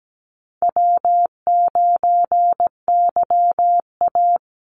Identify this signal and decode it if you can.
Morse code, heard as a keyed tone, about 17 wpm, 700 Hz. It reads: W9YA
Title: Easy Call Signs - 17wpm